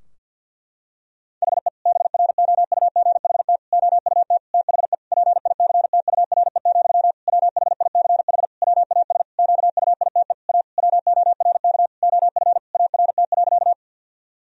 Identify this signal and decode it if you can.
Morse code, heard as a keyed tone, about 50 wpm, 700 Hz. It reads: HE BROUGHT OUT THE PICTURE, WHICH WAS QUITE A WORK OF ART.
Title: Sentences from Top 1000 Words - Review Part 3 - 50wpm